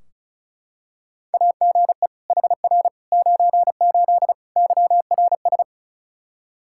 Morse code, heard as a keyed tone, about 36 wpm, 700 Hz. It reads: AGE HR 98 YRS
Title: K1USN SST QSO Practice 35wpm